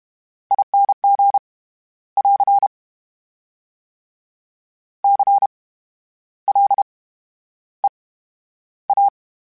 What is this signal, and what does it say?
Morse code, heard as a keyed tone, about 32 wpm, spaced as for 11 wpm, 800 Hz. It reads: ING <AR> CLEA
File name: Special INGs / Special INGs-mp3s / VerySlowToFaster Ing @30